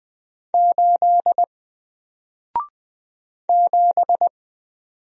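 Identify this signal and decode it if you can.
Morse code, heard as a keyed tone, about 20 wpm, 700 Hz. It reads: 8 7
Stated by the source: Single letter and number Morse drill at 20 wpm, from N4LCD.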